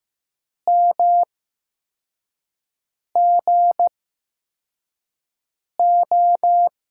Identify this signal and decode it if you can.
Morse code, heard as a keyed tone, about 15 wpm, spaced as for 4 wpm, 700 Hz. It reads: MGO
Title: USA Call Signs - 5 Characters - 8x Character Spacing - 15wpm